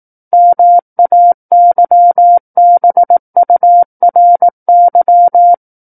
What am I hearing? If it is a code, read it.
Morse code, heard as a keyed tone, about 18 wpm, 700 Hz. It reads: MAYBURY
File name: 18 WPM